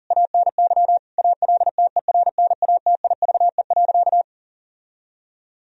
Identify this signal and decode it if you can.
Morse code, heard as a keyed tone, about 40 wpm, 700 Hz. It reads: ANY ALTERNATIVE.
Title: Sentences from Top 1000 Words - Review Part 4 - 40wpm